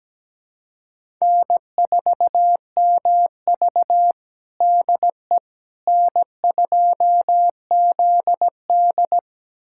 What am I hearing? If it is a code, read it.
Morse code, heard as a keyed tone, about 17 wpm, 700 Hz. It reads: N4MV DE N2ZD
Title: USA Callsign DE Callsign - 4 Characters - 17wpm